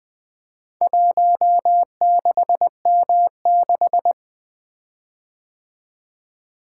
Morse code, heard as a keyed tone, about 20 wpm, 700 Hz. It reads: 16M6
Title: Four Letter-Number Combinations - 20wpm